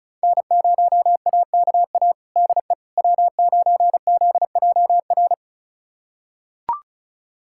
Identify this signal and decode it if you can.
Morse code, heard as a keyed tone, about 36 wpm, 700 Hz. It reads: N0AKA DE W9ZJR
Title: USA Callsign DE Callsign - 5 Characters - 35wpm